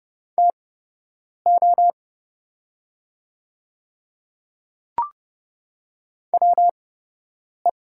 Morse code, heard as a keyed tone, about 30 wpm, spaced as for 8 wpm, 700 Hz. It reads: TO WE
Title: 8x Character Spacing - Top 100 Words - 30wpm